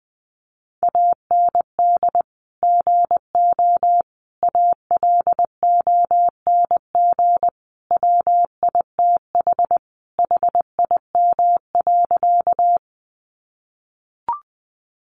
Morse code, heard as a keyed tone, about 20 wpm, 700 Hz. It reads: AND GO ALONG WITH HIM.
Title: Sentences from Top 700 Words 20wpm